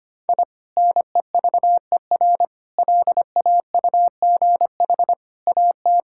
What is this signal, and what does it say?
Morse code, heard as a keyed tone, about 25 wpm, 700 Hz. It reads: I NEVER LAUGH AT
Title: Sentences from Top 1000 Words - Review Part 6 - 25wpm